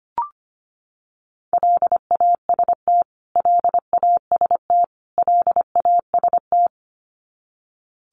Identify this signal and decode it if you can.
Morse code, heard as a keyed tone, about 25 wpm, 700 Hz. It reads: LAST LAST LAST
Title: Top 500 Words Repeated 25wpm